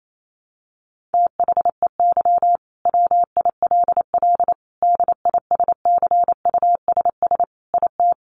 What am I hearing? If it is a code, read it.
Morse code, heard as a keyed tone, about 28 wpm, 700 Hz. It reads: THEY WILL DISCUSS IT
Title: Sentences from Top 1400 Words 28wpm